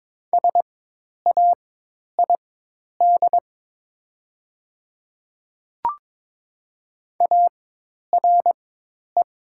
Morse code, heard as a keyed tone, about 22 wpm, spaced as for 10 wpm, 690 Hz. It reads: SAID ARE
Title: 4x Character Spacing - Top 100 Words - 22wpm